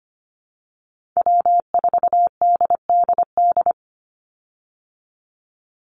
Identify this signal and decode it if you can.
Morse code, heard as a keyed tone, about 25 wpm, 700 Hz. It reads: W4DDD
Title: Call Signs Repeated 2x - 25wpm